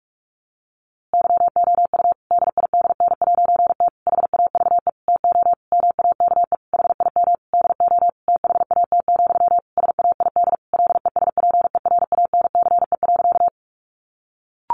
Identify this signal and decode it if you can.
Morse code, heard as a keyed tone, about 44 wpm, 700 Hz. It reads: YOU DIDN'T HAVE TO MAKE HIM DO THAT, SAID LESPERANCE.
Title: A Sound Of Thunder 45wpm